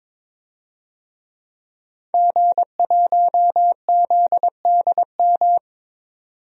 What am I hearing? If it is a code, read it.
Morse code, heard as a keyed tone, about 22 wpm, 700 Hz. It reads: G1ZDM
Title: DXCC-100 Call Signs - 5 Characters - 22wpm